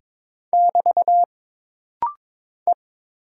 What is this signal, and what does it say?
Morse code, heard as a keyed tone, about 22 wpm, 700 Hz. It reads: <BT> E
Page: This rapid-fire timing is geared to push your ICR (Instant Character Recognition) skills.